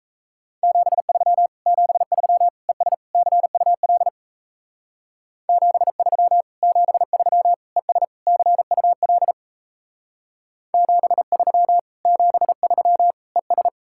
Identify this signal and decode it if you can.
Morse code, heard as a keyed tone, about 34 wpm, 700 Hz. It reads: 73 73 ES CUL 73 73 ES CUL 73 73 ES
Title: Speed-Racing Common QSO Phrases - 28wpm